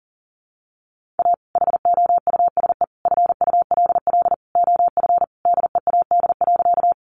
Morse code, heard as a keyed tone, about 40 wpm, 700 Hz. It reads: A HOUSE FULL OF DEAD.
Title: Sentences from Top 1500 Words 40wpm